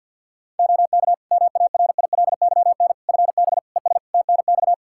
Morse code, heard as a keyed tone, about 48 wpm, 700 Hz. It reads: OK MARILYN FB ES TNX
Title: Practice QSO - Exchange 2 - Info - 50wpm